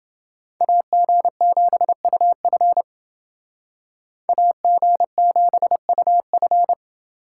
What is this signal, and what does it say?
Morse code, heard as a keyed tone, about 27 wpm, 700 Hz. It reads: AG7UF AG7UF